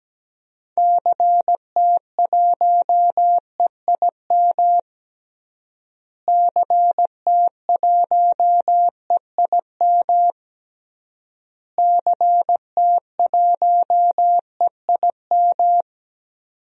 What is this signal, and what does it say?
Morse code, heard as a keyed tone, about 17 wpm, 700 Hz. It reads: CT1EIM CT1EIM CT1EIM
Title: DXCC-100 Call Signs - 6 Characters - Repeated 3x - 17wpm